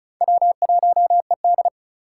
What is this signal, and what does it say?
Morse code, heard as a keyed tone, about 36 wpm, 700 Hz. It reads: W1ED
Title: USA Call Signs - 4 Characters - 2x Character Spacing - 35wpm